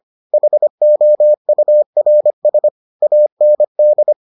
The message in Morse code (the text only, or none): HOURS AND